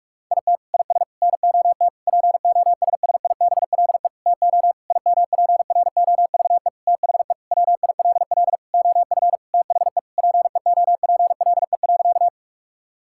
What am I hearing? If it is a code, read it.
Morse code, heard as a keyed tone, about 41 wpm, 700 Hz. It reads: IT IS NOT POSSIBLE TO IMPROVE THE WILL OF THE PEOPLE.
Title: Sentences from Top 1000 Words - Review Part 5 - 45wpm